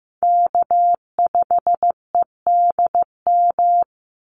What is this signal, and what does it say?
Morse code, heard as a keyed tone, about 15 wpm, 700 Hz. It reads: K5EDM
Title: Call Signs Repeated 3x - 15wpm